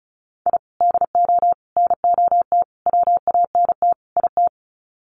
Morse code, heard as a keyed tone, about 34 wpm, 700 Hz. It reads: I DO NOT WANT IT
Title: Sets of 5 Words Encore2 35wpm